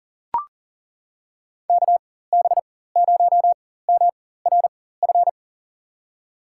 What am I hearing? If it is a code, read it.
Morse code, heard as a keyed tone, about 39 wpm, spaced as for 18 wpm, 700 Hz. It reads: KB0MRF
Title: USA Call Signs - 6 Characters - 4x Character Spacing - 40wpm